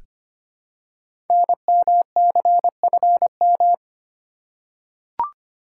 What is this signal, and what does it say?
Morse code, heard as a keyed tone, about 25 wpm, 700 Hz. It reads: NMCFM
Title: Code Groups - 5 Letters - 25wpm